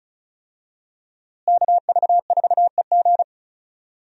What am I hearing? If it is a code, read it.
Morse code, heard as a keyed tone, about 34 wpm, 700 Hz. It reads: KV4EG
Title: USA Call Signs - 5 Characters - Repeated 3x - 35wpm